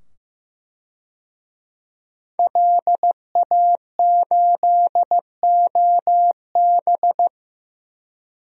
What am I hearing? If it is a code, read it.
Morse code, heard as a keyed tone, about 15 wpm, 700 Hz. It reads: LA8OB
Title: DXCC-100 Call Signs - 5 Characters - 15wpm